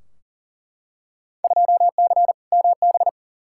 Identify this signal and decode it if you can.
Morse code, heard as a keyed tone, about 38 wpm, 700 Hz. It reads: 2C MB
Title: Field Day Exchanges Encore - 40wpm